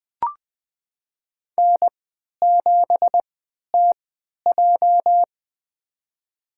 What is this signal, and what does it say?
Morse code, heard as a keyed tone, about 20 wpm, spaced as for 11 wpm, 700 Hz. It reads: N7TJ
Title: USA Call Signs - 4 Characters - 3x Character Spacing - 20wpm